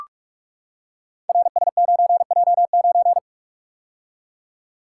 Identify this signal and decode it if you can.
Morse code, heard as a keyed tone, about 44 wpm, 690 Hz. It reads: RS9J9
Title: Code Groups - 5 Letter:Numbers - 45wpm